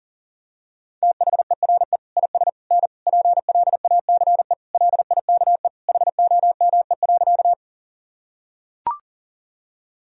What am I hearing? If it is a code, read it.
Morse code, heard as a keyed tone, about 40 wpm, 700 Hz. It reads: THERE IS N PLACE LIKE HOME.
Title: Sentences from Top 200 Words and Abbreviations 40wpm